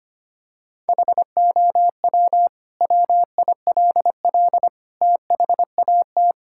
Morse code, heard as a keyed tone, about 25 wpm, 700 Hz. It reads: HOW WILL THAT
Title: Sentences from Top 1000 Words - Review Part 6 - 25wpm